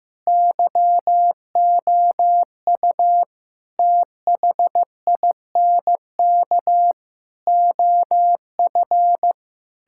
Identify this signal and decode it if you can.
Morse code, heard as a keyed tone, about 16 wpm, 700 Hz. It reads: YOU THINK OF
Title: Sentences from Top 1600 Words 15wpm